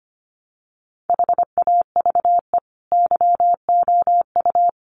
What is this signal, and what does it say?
Morse code, heard as a keyed tone, about 25 wpm, 700 Hz. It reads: HAVE YOU
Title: Sentences from Top 1100 Words 25wpm